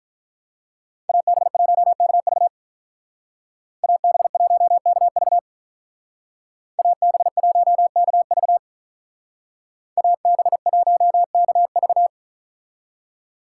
Morse code, heard as a keyed tone, about 43 wpm, 700 Hz. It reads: AB1KV AB1KV AB1KV AB1KV
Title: Speed Racing - USA Call Signs 35wpm